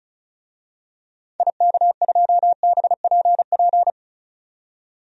Morse code, heard as a keyed tone, about 36 wpm, 700 Hz. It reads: IK2BPP
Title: DXCC-100 Call Signs - 6 Characters - 35wpm